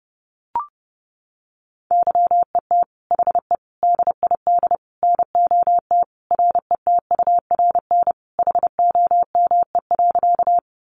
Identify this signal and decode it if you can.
Morse code, heard as a keyed tone, about 30 wpm, 700 Hz. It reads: YET HE DID NOT RETURN HOME.
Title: Sentences from Top 800 Words 30wpm